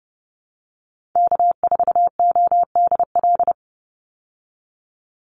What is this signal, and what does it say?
Morse code, heard as a keyed tone, about 28 wpm, 700 Hz. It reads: K4ODL
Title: Call Signs Repeated 3x - 30wpm